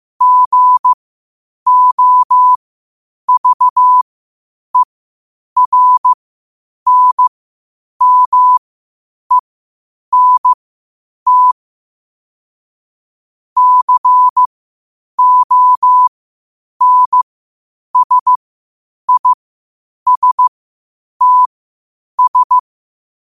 Morse code, heard as a keyed tone, about 15 wpm, spaced as for 8 wpm, 1000 Hz. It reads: GOVERNMENT CONSISTS
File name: 8wpm